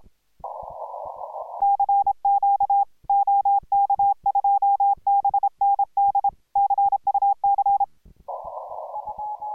Morse code, heard as a keyed tone, about 26 wpm, 810 Hz. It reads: CQ OK2BND CUC